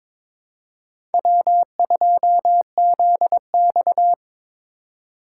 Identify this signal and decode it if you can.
Morse code, heard as a keyed tone, about 22 wpm, 700 Hz. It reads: W2ZX
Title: USA Call Signs - 4 Characters - 3x Character Spacing - 22wpm